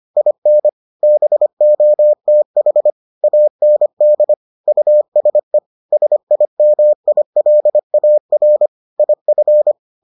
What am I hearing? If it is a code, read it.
Morse code, heard as a keyed tone, about 25 wpm, 600 Hz. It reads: IN BOTH AND USE SIMILAR IF